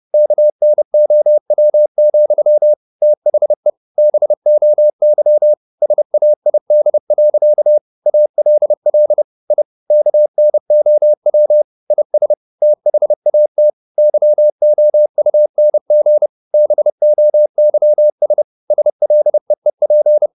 Morse code, heard as a keyed tone, about 30 wpm, 600 Hz. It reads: KNOW, THE BOY SAID. ALL I KNOW IS THAT YOUNG BOYS SLEEP
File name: the old man and the sea / the old man and the sea 1 30wpm